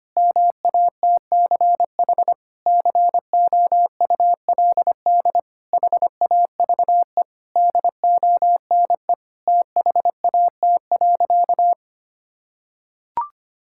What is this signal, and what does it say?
Morse code, heard as a keyed tone, about 25 wpm, 700 Hz. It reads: MATCH COULD HAVE DONE THAT.
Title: Sentences from Top 1600 Words 25wpm